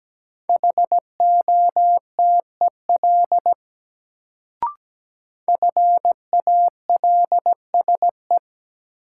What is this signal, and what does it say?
Morse code, heard as a keyed tone, about 17 wpm, 700 Hz. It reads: HOTEL FALSE
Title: Sentences from Top 1600 Words - New Word Drill - 17wpm